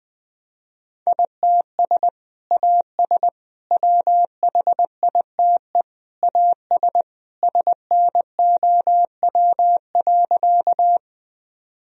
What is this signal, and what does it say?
Morse code, heard as a keyed tone, about 21 wpm, 700 Hz. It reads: ITS AS WHITE AS SNOW.
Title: Sentences from Top 1000 Words - Review Part 3 - 20wpm